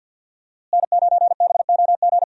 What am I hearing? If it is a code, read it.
Morse code, heard as a keyed tone, about 46 wpm, 700 Hz. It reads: N9BOG
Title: USA Call Signs - 5 Characters - Repeated 3x - 50wpm